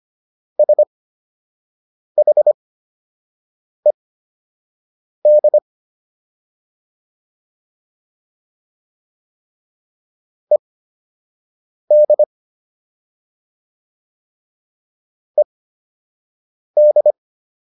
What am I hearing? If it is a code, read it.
Morse code, heard as a keyed tone, about 25 wpm, spaced as for 6 wpm, 600 Hz. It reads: SHED ED ED